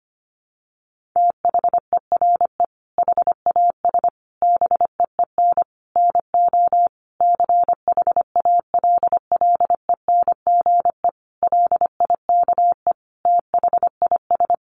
Morse code, heard as a keyed tone, about 25 wpm, 700 Hz. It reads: THERE HAS BEEN NO CHALLENGE LIKE THIS
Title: Sentences from Top 1500 Words 25wpm